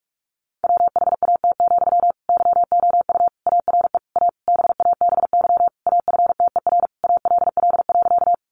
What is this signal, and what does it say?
Morse code, heard as a keyed tone, about 44 wpm, 700 Hz. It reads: WHAT, YOU ARE A BABY AFTER ALL.
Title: Sentences from Top 1300 Words 45wpm